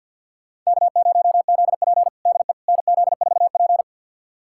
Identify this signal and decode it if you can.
Morse code, heard as a keyed tone, about 52 wpm, 700 Hz. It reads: K0ZP DE NZ4P
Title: USA Callsign DE Callsign - 4 Characters - 50wpm